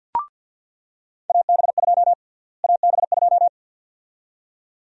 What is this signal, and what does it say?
Morse code, heard as a keyed tone, about 47 wpm, 700 Hz. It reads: AB2 AB2
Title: USA Call Sign Prefixes - Repeated 2x - 50wpm